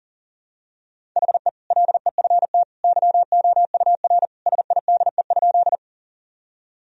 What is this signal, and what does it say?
Morse code, heard as a keyed tone, about 40 wpm, 700 Hz. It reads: HE LEFT YOUR SIDE?
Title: Sentences from Top 1000 Words - Review Part 1 - 40wpm